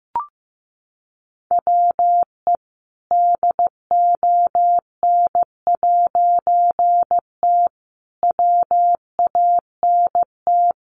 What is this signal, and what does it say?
Morse code, heard as a keyed tone, about 15 wpm, 700 Hz. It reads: WE DON'T WANT
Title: A Sound Of Thunder 15wpm